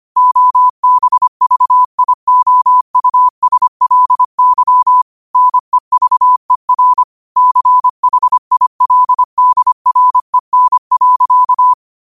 Morse code, heard as a keyed tone, about 25 wpm, 1000 Hz. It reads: OBVIOUSLY NEVER CHILDREN.
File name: qotdfiles / 25wpm